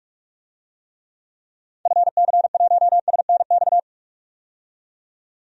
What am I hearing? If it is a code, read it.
Morse code, heard as a keyed tone, about 46 wpm, 700 Hz. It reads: FC1SNX
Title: DXCC-100 Call Signs - 6 Characters - 45wpm